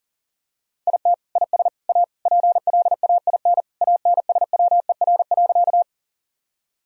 Morse code, heard as a keyed tone, about 40 wpm, 700 Hz. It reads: IT IS A PLAIN ANSWER.
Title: Sentences from Top 1000 Words - Review Part 2 - 40wpm